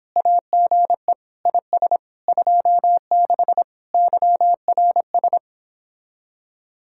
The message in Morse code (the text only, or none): AGE IS 26 YRS